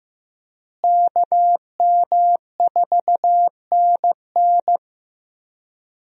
Morse code, heard as a keyed tone, about 15 wpm, 700 Hz. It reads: KM4NN
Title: USA Call Signs - 5 Characters - Repeated 2x - 15wpm